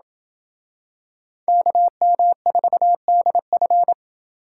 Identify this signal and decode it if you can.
Morse code, heard as a keyed tone, about 25 wpm, 700 Hz. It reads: KM4DF
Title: Speed Racing - USA 2x2 Call Signs 20wpm